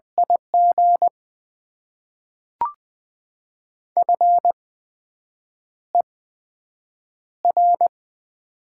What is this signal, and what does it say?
Morse code, heard as a keyed tone, about 20 wpm, spaced as for 7 wpm, 700 Hz. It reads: IG FER
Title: Top 20 QSO Elements - 8x Character Spacing - 20wpm